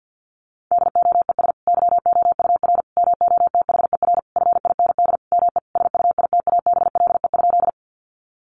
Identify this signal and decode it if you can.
Morse code, heard as a keyed tone, about 50 wpm, 700 Hz. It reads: DOES YOUR MOTHER FIND ME SUITABLE?
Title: Sentences from Top 1400 Words 50wpm